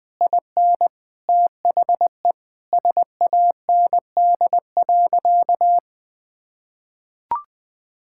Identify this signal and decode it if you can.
Morse code, heard as a keyed tone, about 20 wpm, 700 Hz. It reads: IN THE SAND.
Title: Sentences with 7-Word Phrases - 20wpm